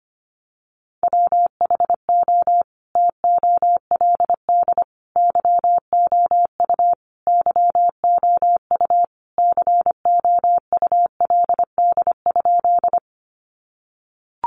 Morse code, heard as a keyed tone, about 25 wpm, 700 Hz. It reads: WHO TOLD YOU YOU COULD?
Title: Sentences from Top 400 Words 25wpm